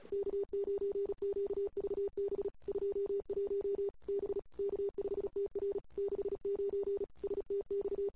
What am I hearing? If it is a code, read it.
Morse code, heard as a keyed tone, about 37 wpm, 400 Hz. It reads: K9QVB 21 B K5TR 69 STX
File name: CW / 2008-ss-cw-k5tr-op-n5rz -21:00